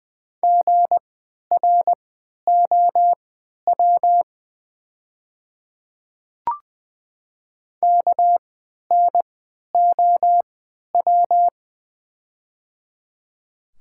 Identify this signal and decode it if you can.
Morse code, heard as a keyed tone, about 20 wpm, spaced as for 11 wpm, 700 Hz. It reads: GROW KNOW
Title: 3x Character Spacing - Top 300 Words - 20wpm